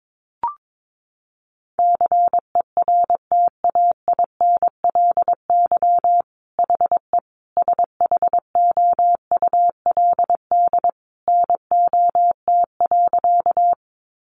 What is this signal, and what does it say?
Morse code, heard as a keyed tone, about 22 wpm, 700 Hz. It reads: CERTAINLY HE SHOULD NOT.
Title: Sentences from Top 900 Words 22wpm